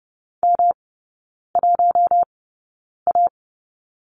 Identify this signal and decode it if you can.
Morse code, heard as a keyed tone, about 30 wpm, spaced as for 9 wpm, 700 Hz. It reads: M1A
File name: Lesson 18e - Letter m - Three Characters 30wpm